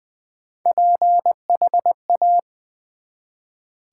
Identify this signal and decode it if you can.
Morse code, heard as a keyed tone, about 20 wpm, 700 Hz. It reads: PHA
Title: Build A Word - 1 Letter At A Time - 3 to 5-Letter Words - 20wpm